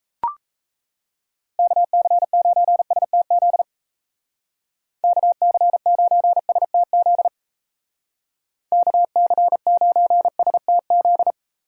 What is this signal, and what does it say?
Morse code, heard as a keyed tone, about 33 wpm, 700 Hz. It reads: KC9STZ KC9STZ KC9STZ
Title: Speed Racing - USA 2x3 Call Signs 28wpm